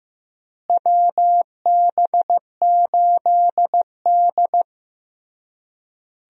Morse code, heard as a keyed tone, about 15 wpm, 700 Hz. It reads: WB8D
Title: USA Call Signs - 4 Characters - Repeated 3x - 15wpm